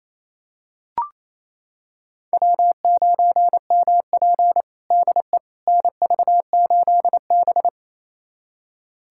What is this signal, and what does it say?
Morse code, heard as a keyed tone, about 28 wpm, 700 Hz. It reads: W9MP DE NV8B
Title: USA Callsign DE Callsign - 4 Characters - 28wpm